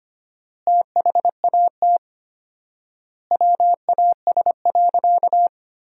Morse code, heard as a keyed tone, about 25 wpm, 700 Hz. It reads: THAT WAS.
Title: 4x Word-Spacing - Sentences from Top 300 Words - 25wpm